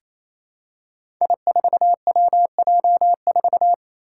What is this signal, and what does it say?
Morse code, heard as a keyed tone, about 27 wpm, 700 Hz. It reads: I4WJ4
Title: Five Challenging Letter-Number Combinations 28wpm